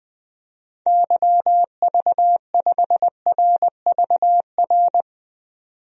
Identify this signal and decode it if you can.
Morse code, heard as a keyed tone, about 20 wpm, 700 Hz. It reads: YV5RVR
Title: DXCC-100 Call Signs - 6 Characters - 20wpm